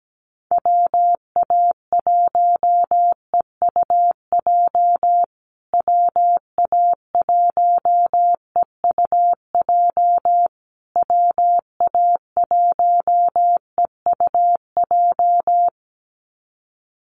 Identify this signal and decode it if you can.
Morse code, heard as a keyed tone, about 17 wpm, 700 Hz. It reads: WA1EUJ WA1EUJ WA1EUJ
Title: USA Call Signs Repeated 3x 17wpm